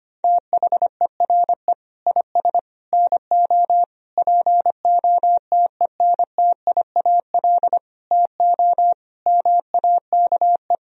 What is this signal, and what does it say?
Morse code, heard as a keyed tone, about 25 wpm, 700 Hz. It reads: THERE IS NO POTENTIAL TO MAKE
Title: Sentences from Top 1000 Words - Review Part 4 - 25wpm